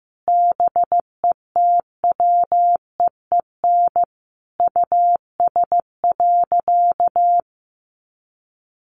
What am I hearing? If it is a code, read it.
Morse code, heard as a keyed tone, about 15 wpm, 700 Hz. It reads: BETWEEN US.
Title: Sentences from Top 500 Words 15wpm